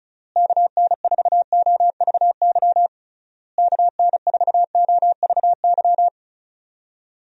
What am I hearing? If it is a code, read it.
Morse code, heard as a keyed tone, about 34 wpm, 700 Hz. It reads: KN4OVY KN4OVY
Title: USA Call Signs - 6 Characters - Repeated 3x - 35wpm